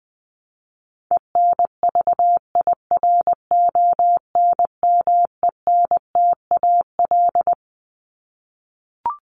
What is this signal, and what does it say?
Morse code, heard as a keyed tone, about 20 wpm, 700 Hz. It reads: ENVIRONMENTAL
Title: Top 1001-1250 Words 20wpm